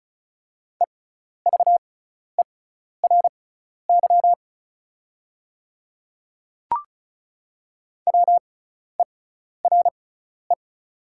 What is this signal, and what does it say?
Morse code, heard as a keyed tone, about 34 wpm, spaced as for 12 wpm, 700 Hz. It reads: EVERY WERE
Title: 6x Character Spacing - Top 300 Words - 35wpm